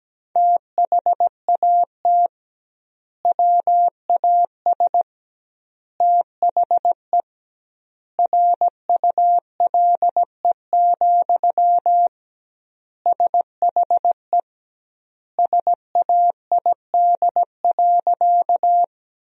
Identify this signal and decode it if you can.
Morse code, heard as a keyed tone, about 17 wpm, 700 Hz. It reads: THAT WAS THE RULE, SHE SAID.
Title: 2x Word-Spacing - Sentences from Top 400 Words - 17wpm